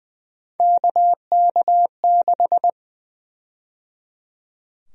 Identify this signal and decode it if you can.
Morse code, heard as a keyed tone, about 20 wpm, 700 Hz. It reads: KK6
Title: USA Call Sign Prefixes - 20wpm